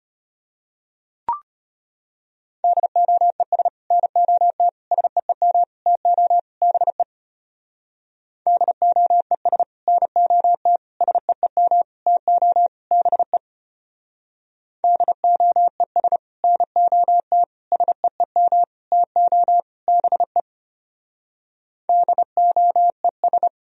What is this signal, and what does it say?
Morse code, heard as a keyed tone, about 33 wpm, 700 Hz. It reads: DOES NOT SEEM TO BE DOES NOT SEEM TO BE DOES NOT SEEM TO BE DOES